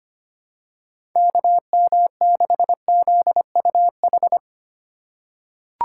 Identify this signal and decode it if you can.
Morse code, heard as a keyed tone, about 25 wpm, 690 Hz. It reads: KM6ZUH
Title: USA Call Signs - 6 Characters - 2x Character Spacing - 25wpm